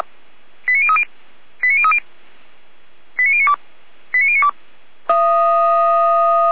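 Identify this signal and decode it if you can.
Five-tone selective call (ZVEI), heard as quick runs of five tones, tones of about 80 ms. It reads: 89030 89030 89092 89092
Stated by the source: ZVEI five-tone call received by SDR.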